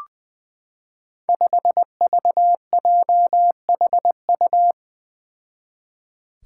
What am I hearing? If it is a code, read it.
Morse code, heard as a keyed tone, about 20 wpm, 700 Hz. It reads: 5VJHU
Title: Five Challenging Letter-Number Combinations 20wpm